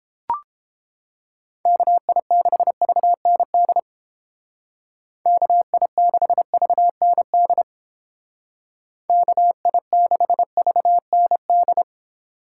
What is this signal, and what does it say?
Morse code, heard as a keyed tone, about 31 wpm, 700 Hz. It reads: KI6VND KI6VND KI6VND